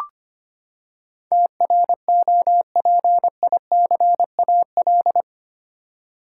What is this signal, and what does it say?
Morse code, heard as a keyed tone, about 25 wpm, 700 Hz. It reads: TROPICAL
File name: Eight Letter Words - 25wpm